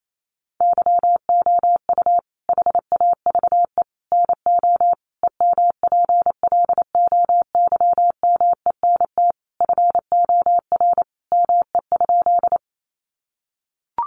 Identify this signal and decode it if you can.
Morse code, heard as a keyed tone, about 28 wpm, 700 Hz. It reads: YOU HAVE NO EMPLOYMENT FOR ME?
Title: Sentences from Top 1500 Words 28wpm